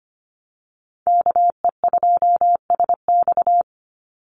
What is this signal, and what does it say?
Morse code, heard as a keyed tone, about 25 wpm, 700 Hz. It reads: KE2SX
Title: Call Signs - 25wpm